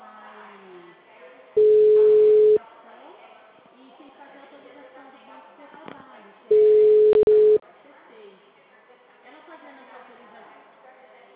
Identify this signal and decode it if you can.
Ringback tone (Europe, 425 Hz), with speech.
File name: out- -2016- - - .6950